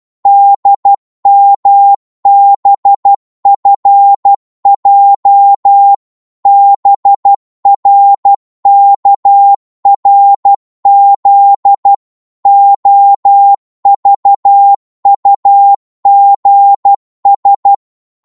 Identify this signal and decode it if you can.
Morse code, heard as a keyed tone, about 12 wpm, 800 Hz. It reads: DMBFJ BRKRZ OVUGS
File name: Morse / ltr-12wpm